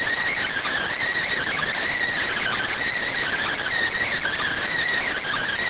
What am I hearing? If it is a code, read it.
Emergency Alert System (SAME) tones (header bursts), heard as harsh buzzing bursts. It is heard from 0.3 to 4.6 s.